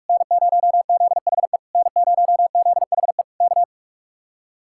Morse code, heard as a keyed tone, about 59 wpm, spaced as for 51 wpm, 700 Hz. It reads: N0ZHE N0ZHE K
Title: Speed-Racing Common QSO Phrases - 45wpm